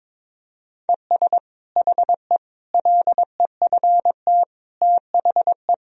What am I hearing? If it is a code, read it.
Morse code, heard as a keyed tone, about 22 wpm, 700 Hz. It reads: ES HE LEFT THE
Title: Sentences from Top 300 Words and Abbreviations 22wpm